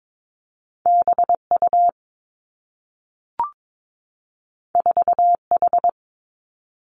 Morse code, heard as a keyed tone, about 22 wpm, 700 Hz. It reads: B U 4 H
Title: Two Challenging Letter-Number Combinations 22wpm